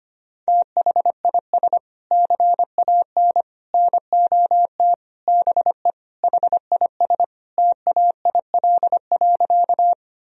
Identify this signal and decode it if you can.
Morse code, heard as a keyed tone, about 25 wpm, 700 Hz. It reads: THIS CAN NOT BE HIS TAIL.
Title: Sentences from Top 1000 Words - Review Part 3 - 25wpm